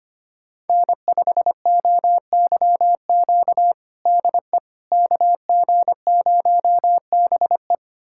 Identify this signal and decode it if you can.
Morse code, heard as a keyed tone, about 25 wpm, 700 Hz. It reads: N5OYQ DE KG0BE
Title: USA Callsign DE Callsign 25wpm